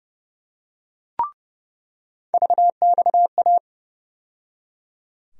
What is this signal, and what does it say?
Morse code, heard as a keyed tone, about 30 wpm, 700 Hz. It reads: VXA
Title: USA Call Sign Postfixes - 3 Letters 30wpm